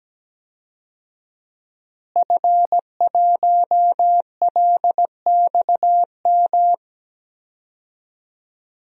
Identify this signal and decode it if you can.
Morse code, heard as a keyed tone, about 17 wpm, 700 Hz. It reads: F1LXM
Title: DXCC-100 Call Signs - 5 Characters - Repeated 2x - 17wpm